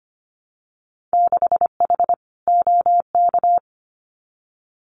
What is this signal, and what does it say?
Morse code, heard as a keyed tone, about 25 wpm, 700 Hz. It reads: 6H OK
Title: Winter Field Day Exchanges Encore 25wpm